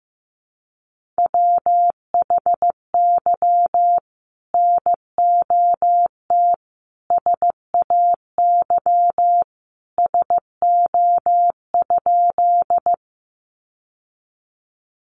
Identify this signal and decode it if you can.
Morse code, heard as a keyed tone, about 15 wpm, 700 Hz. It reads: WHY NOT SAY SO?
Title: Sentences from Top 600 Words 15wpm